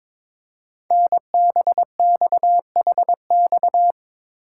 Morse code, heard as a keyed tone, about 22 wpm, 700 Hz. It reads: NBXHX
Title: Code Groups - 5 Letters - 22wpm